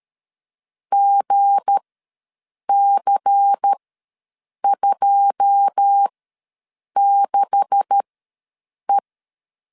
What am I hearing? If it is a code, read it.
Morse code, heard as a keyed tone, about 13 wpm, spaced as for 7 wpm, 800 Hz. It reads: GC26E
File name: LEZIONI / 10 SIGNS / 10A SLASH ws 12wpm 3443